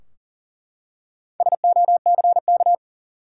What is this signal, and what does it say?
Morse code, heard as a keyed tone, about 40 wpm, 700 Hz. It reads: SOCK
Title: Four Letter Words - 40wpm